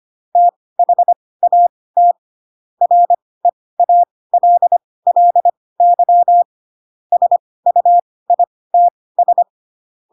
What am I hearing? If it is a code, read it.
Morse code, heard as a keyed tone, about 25 wpm, spaced as for 18 wpm, 700 Hz. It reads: THAT REALLY SUITS
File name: Pinocchio / Pinocchio ch4 12 25 ik1hge